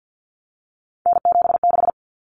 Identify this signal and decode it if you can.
Morse code, heard as a keyed tone, about 50 wpm, 700 Hz. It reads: N76
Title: Common Cut Numbers - 3-Digit 50wpm